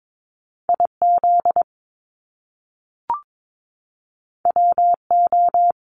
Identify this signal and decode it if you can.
Morse code, heard as a keyed tone, about 22 wpm, 700 Hz. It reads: IZ WO
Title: Top Two Letter Combinations in Words 22wpm